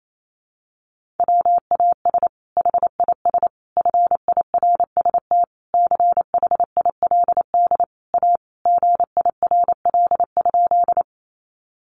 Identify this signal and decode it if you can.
Morse code, heard as a keyed tone, about 28 wpm, 700 Hz. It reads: WAS HIS FIRST CHILD A GIRL?
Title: Sentences from Top 800 Words 28wpm